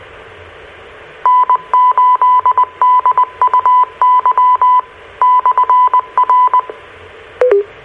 Morse code, heard as a keyed tone, about 20 wpm, 1000 Hz. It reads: N8DUY/R
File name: RR Skywarn / Washtenaw County SKYWARN N8DUY and W8SRC Repeaters